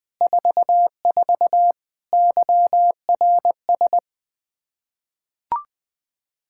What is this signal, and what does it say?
Morse code, heard as a keyed tone, about 20 wpm, 700 Hz. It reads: 44 YRS
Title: Practice QSO - Exchange 3 - More Info - 20wpm